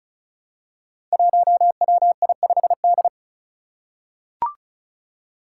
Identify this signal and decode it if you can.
Morse code, heard as a keyed tone, about 36 wpm, 700 Hz. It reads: 1WI5D
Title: Five Challenging Letter-Number Combinations 35wpm